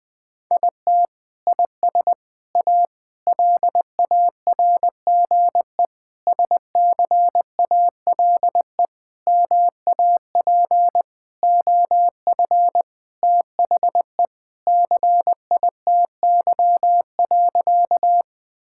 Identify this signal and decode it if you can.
Morse code, heard as a keyed tone, about 21 wpm, 700 Hz. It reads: IT IS A LARGE SCALE MAP OF THE CITY.
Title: Sentences from Top 1000 Words - Review Part 5 - 20wpm